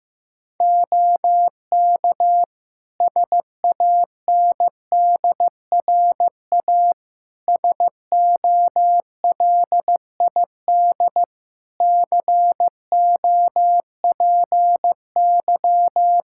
Morse code, heard as a keyed tone, about 15 wpm, 700 Hz. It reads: OK SANDRA SOLID COPY
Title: Practice QSO - Exchange 3 - More Info - 15wpm